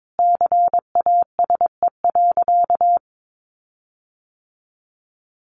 Morse code, heard as a keyed tone, about 22 wpm, 700 Hz. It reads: CASE.
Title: Sentences from Top 1300 Words 22wpm